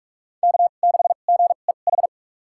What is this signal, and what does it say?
Morse code, heard as a keyed tone, about 47 wpm, spaced as for 33 wpm, 690 Hz. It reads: K6GEH
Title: USA Call Signs - 5 Characters - 2x Character Spacing - 45wpm